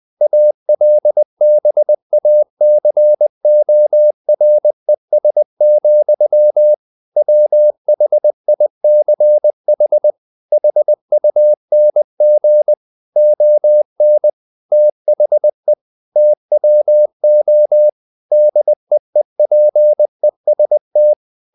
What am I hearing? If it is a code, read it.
Morse code, heard as a keyed tone, about 20 wpm, 600 Hz. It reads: ALBACORES, WHICH HUNG ON THE TWO DEEPEST
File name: the old man and the sea / the old man and the sea part 1 20wpm